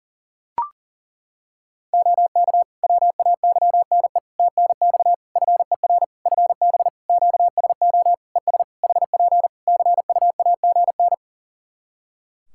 Morse code, heard as a keyed tone, about 40 wpm, 700 Hz. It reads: OK WAYNE TNX FER FB QSO ES HP CUAGN
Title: Practice QSO - Exchange 4 - Ending - 40wpm